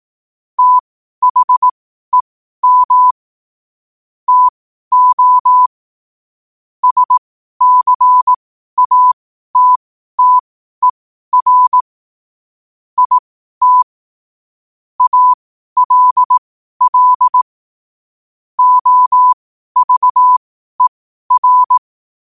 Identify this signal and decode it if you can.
Morse code, heard as a keyed tone, about 18 wpm, spaced as for 12 wpm, 1000 Hz. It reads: THEM TO SCATTER IT ALL OVER
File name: qotdfiles / 12wpm